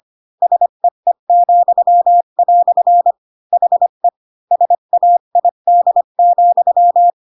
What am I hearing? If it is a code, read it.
Morse code, heard as a keyed tone, about 25 wpm, 700 Hz. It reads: SEE," HE SAID,
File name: Pinocchio / Pinocchio ch1 20 25 ik1hge